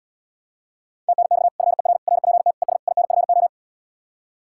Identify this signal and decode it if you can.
Morse code, heard as a keyed tone, about 78 wpm, 700 Hz. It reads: TNX FER RPT ES INFO